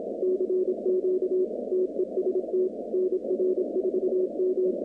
Morse code, heard as a keyed tone, about 26 wpm, 370 Hz, with noise. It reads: CQ TEST NR4M
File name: 2019-cw-wpx-kt5j-k5pi